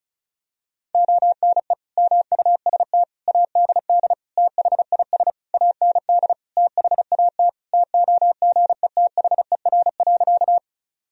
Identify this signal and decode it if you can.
Morse code, heard as a keyed tone, about 34 wpm, 700 Hz. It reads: ONE MUST ADD THIS AND THAT TOGETHER.
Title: Sentences from Top 1000 Words - Review Part 1 - 35wpm